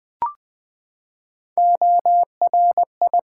Morse code, heard as a keyed tone, about 21 wpm, 700 Hz. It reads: ORI
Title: Build A Word - 3 Letters At A Time - 6-Letter Words - 20wpm